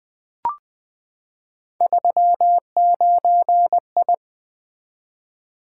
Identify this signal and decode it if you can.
Morse code, heard as a keyed tone, about 20 wpm, 700 Hz. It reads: 39I
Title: Three Letter-Number Combinations - 20wpm